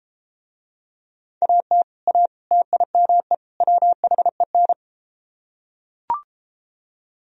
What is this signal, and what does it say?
Morse code, heard as a keyed tone, about 32 wpm, 700 Hz. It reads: AT A TIME WHEN